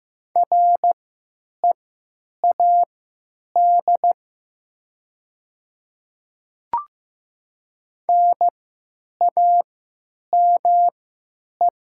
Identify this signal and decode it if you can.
Morse code, heard as a keyed tone, about 15 wpm, spaced as for 9 wpm, 700 Hz. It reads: READ NAME
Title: 3x Character Spacing - Top 200 Words - 15wpm